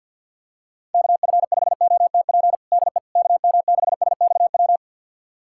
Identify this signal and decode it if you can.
Morse code, heard as a keyed tone, about 52 wpm, 700 Hz. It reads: KF5OTP DE KM6SXW
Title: USA Callsign DE Callsign 50wpm